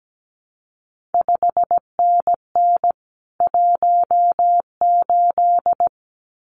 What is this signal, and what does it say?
Morse code, heard as a keyed tone, about 17 wpm, 700 Hz. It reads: 5NN 18
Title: CQ WW DX Contest Exchange Encore - 17wpm